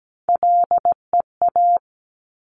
Morse code, heard as a keyed tone, about 17 wpm, 700 Hz. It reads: LEA
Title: Top Three Letter Combinations in Words 17wpm